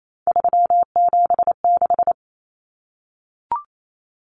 Morse code, heard as a keyed tone, about 28 wpm, 700 Hz. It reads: 376
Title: Common Cut Numbers - 3-Digit 28wpm